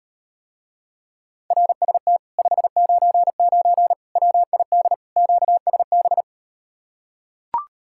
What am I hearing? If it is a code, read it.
Morse code, heard as a keyed tone, about 38 wpm, 700 Hz. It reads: RST 599 WID QSB